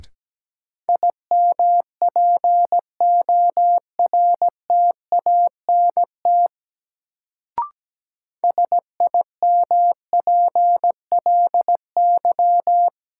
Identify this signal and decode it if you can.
Morse code, heard as a keyed tone, about 17 wpm, 700 Hz. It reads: IMPORTANT SIMPLY
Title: Sentences from Top 600 Words - New Word Drill - 17wpm